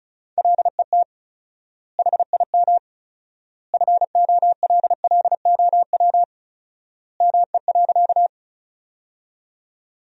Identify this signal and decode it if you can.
Morse code, heard as a keyed tone, about 34 wpm, 700 Hz. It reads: LET HIM FOLLOW ME.
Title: 4x Word-Spacing - Sentences from Top 300 Words - 35wpm